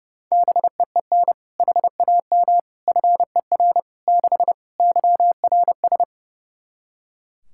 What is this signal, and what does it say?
Morse code, heard as a keyed tone, about 32 wpm, 700 Hz. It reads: BEEN HAM FER 6 YRS
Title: K1USN SST QSO Practice 30wpm